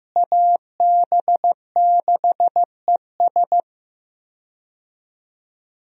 Morse code, heard as a keyed tone, about 15 wpm, 700 Hz. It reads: AB6ES
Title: USA Call Signs - 5 Characters - Repeated 2x - 15wpm